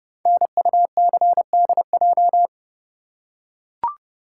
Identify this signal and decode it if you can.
Morse code, heard as a keyed tone, about 30 wpm, 700 Hz. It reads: NUCDJ
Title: Code Groups - 5 Letters - 30wpm